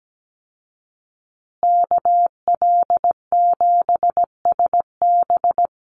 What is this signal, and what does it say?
Morse code, heard as a keyed tone, about 17 wpm, 700 Hz. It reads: KL7SB
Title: Call Signs Repeated 2x - 17wpm